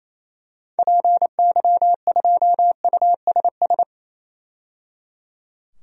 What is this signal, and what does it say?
Morse code, heard as a keyed tone, about 28 wpm, 700 Hz. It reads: PY2USS
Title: DXCC-100 Call Signs - 6 Characters - 28wpm